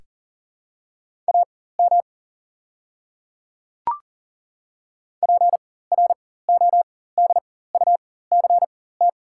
Morse code, heard as a keyed tone, about 40 wpm, spaced as for 19 wpm, 700 Hz. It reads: AM PRODUCT
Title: 4x Character Spacing - Top 500 Words - 40wpm